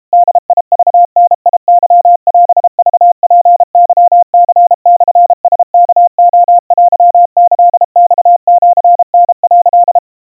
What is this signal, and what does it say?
Morse code (Japanese Wabun code), heard as a keyed tone, about 32 wpm, 700 Hz. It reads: ボクダケオクツケニモラワレテキマシタ」
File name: kawa / Morse sample / kamyu 120 cpm 700 Hz Crispy